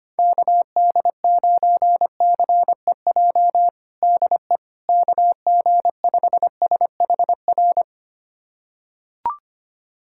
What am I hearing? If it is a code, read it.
Morse code, heard as a keyed tone, about 25 wpm, 700 Hz. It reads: KD9CEJ DE KG5SHR
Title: USA Callsign DE Callsign - 6 Characters - 25wpm